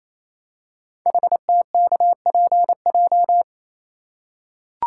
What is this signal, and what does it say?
Morse code, heard as a keyed tone, about 29 wpm, 700 Hz. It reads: HTKPJ
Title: Code Groups - 5 Letter:Numbers - 28wpm